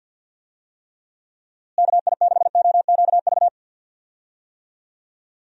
Morse code, heard as a keyed tone, about 51 wpm, 700 Hz. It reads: KI6OQV
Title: USA Call Signs - 6 Characters - Repeated 2x - 50wpm